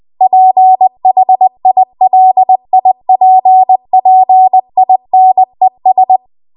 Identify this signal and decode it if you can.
Morse code, heard as a keyed tone, about 20 wpm, 750 Hz. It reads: PHILIPPINES
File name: Archive / 20 WPM / 20WPM